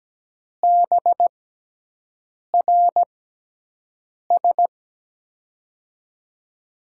Morse code, heard as a keyed tone, about 17 wpm, spaced as for 6 wpm, 700 Hz. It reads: BRS
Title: USA Call Signs - 5 Characters - 6x Character Spacing - 17wpm